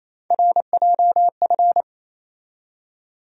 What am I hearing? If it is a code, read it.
Morse code, heard as a keyed tone, about 28 wpm, 700 Hz. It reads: RJF
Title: USA Call Sign Postfixes - 3 Letters 28wpm